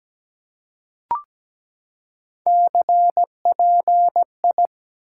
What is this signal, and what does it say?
Morse code, heard as a keyed tone, about 17 wpm, 700 Hz. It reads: CPI
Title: Top 90 QSO Elements 17wpm